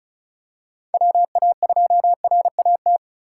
Morse code, heard as a keyed tone, about 36 wpm, 700 Hz. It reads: WA2RAT
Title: USA Call Signs - 6 Characters - 2x Character Spacing - 35wpm